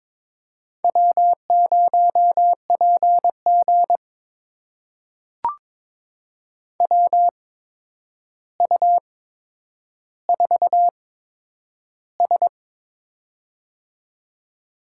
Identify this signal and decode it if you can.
Morse code, heard as a keyed tone, about 22 wpm, spaced as for 9 wpm, 700 Hz. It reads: W0PG WU4S
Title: USA Call Signs - 8x Character Spacing - 22wpm